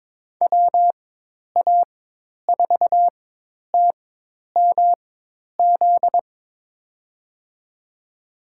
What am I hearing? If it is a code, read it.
Morse code, heard as a keyed tone, about 22 wpm, spaced as for 10 wpm, 700 Hz. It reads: WA4TMZ
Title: USA Call Signs - 6 Characters - 4x Character Spacing - 22wpm